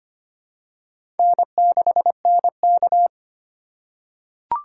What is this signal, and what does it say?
Morse code, heard as a keyed tone, about 25 wpm, 700 Hz. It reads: N6NK
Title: USA Call Signs - 4 Characters - 3x Character Spacing - 25wpm